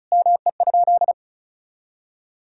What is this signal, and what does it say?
Morse code, heard as a keyed tone, about 36 wpm, 700 Hz. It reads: ME?
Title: Sentences from Top 1600 Words 35wpm